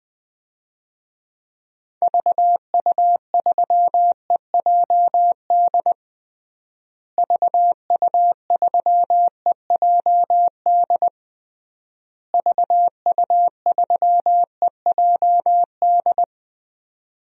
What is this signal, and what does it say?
Morse code, heard as a keyed tone, about 20 wpm, 700 Hz. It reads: VU3EJD VU3EJD VU3EJD
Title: DXCC-100 Call Signs - 6 Characters - Repeated 3x - 20wpm